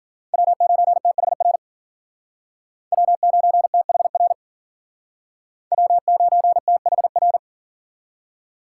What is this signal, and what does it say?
Morse code, heard as a keyed tone, about 48 wpm, 700 Hz. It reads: W9THR W9THR W9THR
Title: Speed Racing - USA 1x3 Call Signs 40wpm